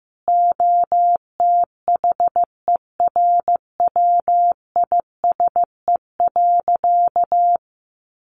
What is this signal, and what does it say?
Morse code, heard as a keyed tone, about 15 wpm, 700 Hz. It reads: OTHERWISE.
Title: Sentences from Top 1200 Words 15wpm